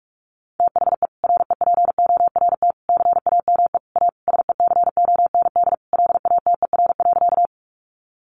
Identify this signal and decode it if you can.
Morse code, heard as a keyed tone, about 44 wpm, 700 Hz. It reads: THE REPORT CAME A SECOND LATER.
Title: Sentences from Top 900 Words 45wpm